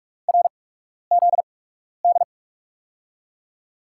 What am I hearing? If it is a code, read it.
Morse code, heard as a keyed tone, about 46 wpm, spaced as for 12 wpm, 700 Hz. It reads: RZD
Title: USA Call Signs - 5 Characters - 8x Character Spacing - 45wpm